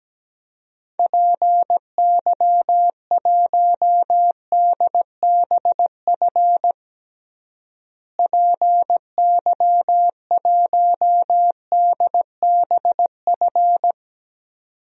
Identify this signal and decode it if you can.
Morse code, heard as a keyed tone, about 17 wpm, 700 Hz. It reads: PY1DBF PY1DBF
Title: DXCC-100 Call Signs - 6 Characters - Repeated 2x - 17wpm